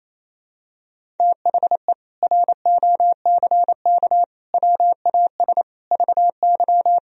Morse code, heard as a keyed tone, about 28 wpm, 700 Hz. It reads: THE ROCK WAS VY
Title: Sentences from Top 400 Words and Abbreviations 28wpm